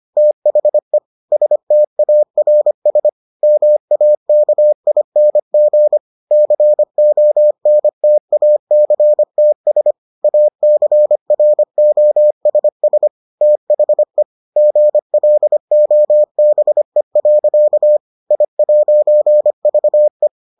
Morse code, heard as a keyed tone, about 25 wpm, 600 Hz. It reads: THE STARS MAKING CONTACTS ACROSS THE GLOBE. I'VE